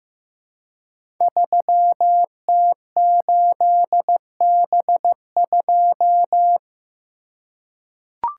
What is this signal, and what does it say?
Morse code, heard as a keyed tone, about 15 wpm, 700 Hz. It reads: 3T8B2
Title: Code Groups - 5 Letter:Numbers - 15wpm